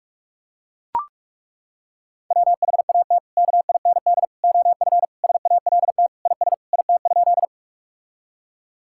Morse code, heard as a keyed tone, about 41 wpm, 700 Hz. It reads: WHAT KIND OF SALT IS IT?
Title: Sentences from Top 1000 Words - Review Part 5 - 45wpm